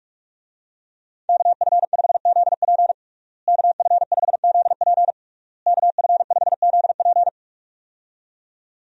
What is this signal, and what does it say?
Morse code, heard as a keyed tone, about 46 wpm, 700 Hz. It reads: KF5ZP KF5ZP KF5ZP
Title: USA Call Signs - 5 Characters - Repeated 3x - 45wpm